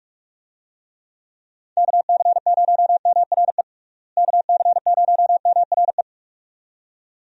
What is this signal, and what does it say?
Morse code, heard as a keyed tone, about 46 wpm, 700 Hz. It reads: KC0MRE KC0MRE
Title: USA Call Signs - 6 Characters - Repeated 2x - 45wpm